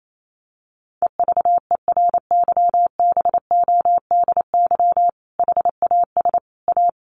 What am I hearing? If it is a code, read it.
Morse code, heard as a keyed tone, about 28 wpm, 700 Hz. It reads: EVERYBODY HAS A
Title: Sentences from Top 1400 Words 28wpm